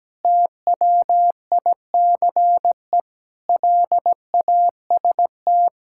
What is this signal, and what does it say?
Morse code, heard as a keyed tone, about 17 wpm, 700 Hz. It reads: TWICE LAST
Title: Sentences from Top 1600 Words 17wpm